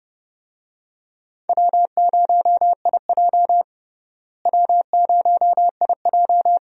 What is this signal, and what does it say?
Morse code, heard as a keyed tone, about 30 wpm, 700 Hz. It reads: W0IJ W0IJ
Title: USA Call Signs - 4 Characters - Repeated 2x - 30wpm